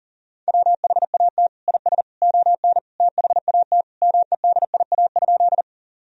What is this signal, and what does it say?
Morse code, heard as a keyed tone, about 40 wpm, 700 Hz. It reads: WHAT IS ON THAT MEDIA?
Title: Sentences from Top 1000 Words - Review Part 4 - 40wpm